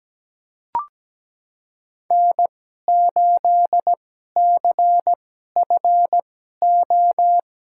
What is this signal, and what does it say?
Morse code, heard as a keyed tone, about 17 wpm, spaced as for 12 wpm, 700 Hz. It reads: N8CFO
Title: USA Call Signs - 5 Characters - 2x Character Spacing - 17wpm